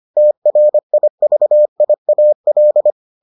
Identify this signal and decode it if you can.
Morse code, heard as a keyed tone, about 25 wpm, 600 Hz. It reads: TRIVIAL